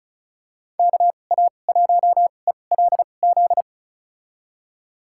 Morse code, heard as a keyed tone, about 36 wpm, spaced as for 25 wpm, 700 Hz. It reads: KA1ELZ
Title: USA Call Signs - 6 Characters - 2x Character Spacing - 35wpm